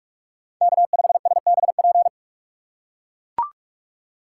Morse code, heard as a keyed tone, about 42 wpm, 700 Hz. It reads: K5SBP